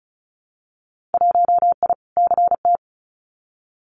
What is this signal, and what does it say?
Morse code, heard as a keyed tone, about 34 wpm, 700 Hz. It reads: 1I CT
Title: Winter Field Day Exchanges 35wpm